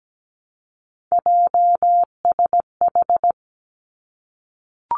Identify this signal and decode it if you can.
Morse code, heard as a keyed tone, about 17 wpm, 700 Hz. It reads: JSH
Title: Three Challenging Letter-Number Combinations 17wpm